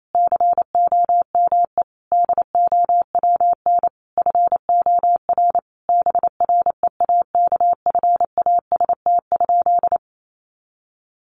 Morse code, heard as a keyed tone, about 28 wpm, 700 Hz. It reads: COME DOWN FOR BREAKFAST?
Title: Sentences from Top 1500 Words 28wpm